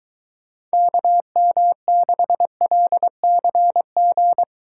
Morse code, heard as a keyed tone, about 23 wpm, 700 Hz. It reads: KM6LCG